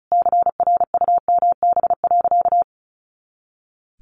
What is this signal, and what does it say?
Morse code, heard as a keyed tone, about 34 wpm, 700 Hz. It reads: CRUMB.
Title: Easy Phrases 35wpm